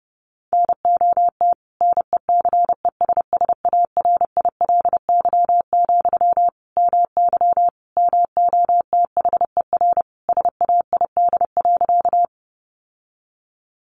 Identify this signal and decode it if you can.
Morse code, heard as a keyed tone, about 30 wpm, 700 Hz. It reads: NOT NECESSARILY, MY MOTHER SAID.
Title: Sentences from Top 1400 Words 30wpm